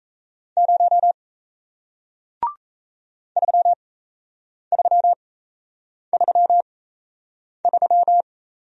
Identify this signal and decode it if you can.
Morse code, heard as a keyed tone, about 39 wpm, spaced as for 8 wpm, 700 Hz. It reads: 0 3333